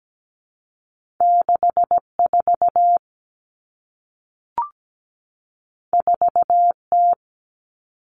Morse code, heard as a keyed tone, about 17 wpm, 700 Hz. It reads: 64 4T
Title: All Cut Numbers - 2-Digit 17wpm